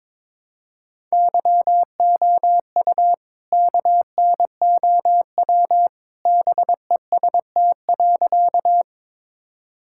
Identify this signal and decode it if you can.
Morse code, heard as a keyed tone, about 22 wpm, 700 Hz. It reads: YOU KNOW BEST.
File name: practice / 22wpm / Sentences with 2-Word Phrases - 22wpm